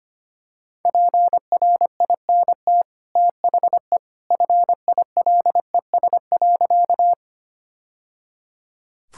Morse code, heard as a keyed tone, about 25 wpm, 690 Hz. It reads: PRINT THE FILES.
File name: Easy Sentences with Hints - 25wpm